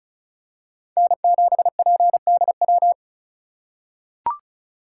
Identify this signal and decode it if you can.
Morse code, heard as a keyed tone, about 36 wpm, 700 Hz. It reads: N7PDW
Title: USA Call Signs - 5 Characters - 4x Character Spacing - 35wpm